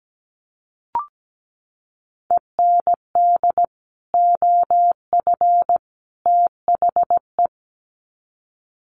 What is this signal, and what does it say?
Morse code, heard as a keyed tone, about 17 wpm, 700 Hz. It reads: END OF THE
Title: Sets of 3 Words 17wpm